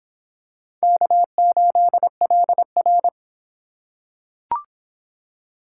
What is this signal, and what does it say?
Morse code, heard as a keyed tone, about 25 wpm, 700 Hz. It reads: K8LR